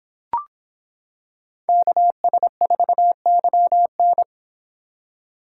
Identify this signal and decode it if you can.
Morse code, heard as a keyed tone, about 25 wpm, 700 Hz. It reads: KS4YN